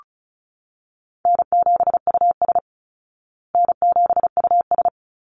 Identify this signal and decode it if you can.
Morse code, heard as a keyed tone, about 34 wpm, 700 Hz. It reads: N7US N7US
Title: Call Signs Repeated 3x - 35wpm